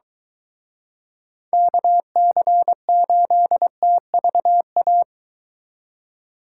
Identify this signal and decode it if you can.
Morse code, heard as a keyed tone, about 23 wpm, 700 Hz. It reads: KC8TVA